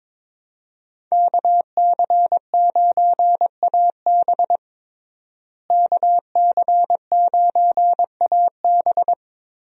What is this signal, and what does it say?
Morse code, heard as a keyed tone, about 22 wpm, 700 Hz. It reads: KC9AB KC9AB
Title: USA Call Signs - 5 Characters - Repeated 3x - 22wpm